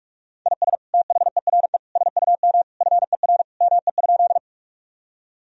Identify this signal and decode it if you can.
Morse code, heard as a keyed tone, about 43 wpm, 700 Hz. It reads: IS THERE SUM FER ME?
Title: Sentences from Top 200 Words and Abbreviations 45wpm